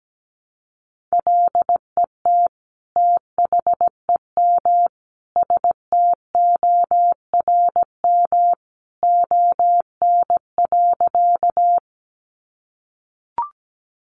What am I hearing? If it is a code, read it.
Morse code, heard as a keyed tone, about 17 wpm, 700 Hz. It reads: LET THEM STORM ON.
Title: Sentences from Top 1500 Words 17wpm